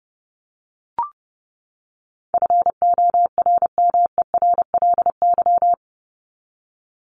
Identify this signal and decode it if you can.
Morse code, heard as a keyed tone, about 30 wpm, 700 Hz. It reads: FORMERLY
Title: LY Suffix - Top 100 Words with Base Words 30wpm